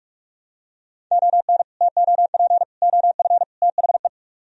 Morse code, heard as a keyed tone, about 44 wpm, 700 Hz. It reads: ON TOP OF THE